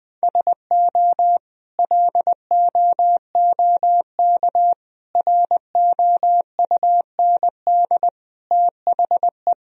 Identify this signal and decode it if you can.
Morse code, heard as a keyed tone, about 20 wpm, 700 Hz. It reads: SO LOOK ROUND THE
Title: Sentences from Top 1000 Words - Review Part 1 - 20wpm